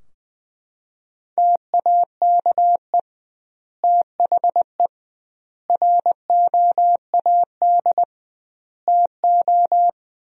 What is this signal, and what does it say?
Morse code, heard as a keyed tone, about 20 wpm, 700 Hz. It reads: TAKE THE ROAD TO
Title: 2x Word-Spacing - Sentences from Top 400 Words - 20wpm